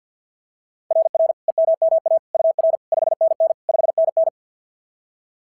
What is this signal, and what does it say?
Morse code, heard as a keyed tone, about 44 wpm, 660 Hz. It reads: RR EMMA UR 5NN 5NN
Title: SOTA QSO Practice 50wpm